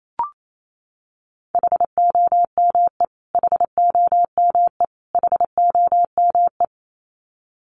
Morse code, heard as a keyed tone, about 28 wpm, 700 Hz. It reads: HOME HOME HOME
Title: Top 500 Words Repeated 28wpm